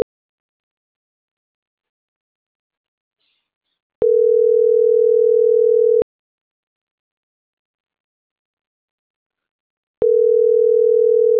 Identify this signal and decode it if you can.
Ringback tone (North America), 440 + 480 Hz.